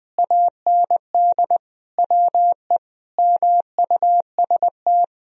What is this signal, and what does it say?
Morse code, heard as a keyed tone, about 20 wpm, 700 Hz. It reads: AND WE MUST
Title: Sentences from Top 1000 Words - Review Part 1 - 20wpm